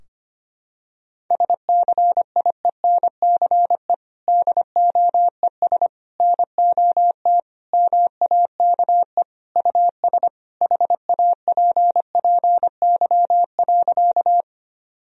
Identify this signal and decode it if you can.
Morse code, heard as a keyed tone, about 25 wpm, 700 Hz. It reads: SCIENCE DOES NOT MAKE US HAPPY.
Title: Sentences from Top 1000 Words - Review Part 2 - 25wpm